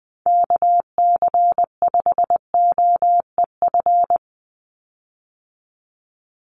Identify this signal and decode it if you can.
Morse code, heard as a keyed tone, about 20 wpm, 700 Hz. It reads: KC5OEF
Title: USA Call Signs 20wpm